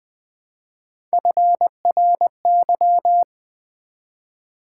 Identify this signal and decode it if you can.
Morse code, heard as a keyed tone, about 20 wpm, 700 Hz. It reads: FRY
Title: Build A Word - 1 Letter At A Time - 3-Letter Words - 20wpm